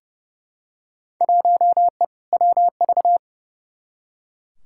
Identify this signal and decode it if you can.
Morse code, heard as a keyed tone, about 30 wpm, 700 Hz. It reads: 1E WV
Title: Field Day Exchanges Encore - 30wpm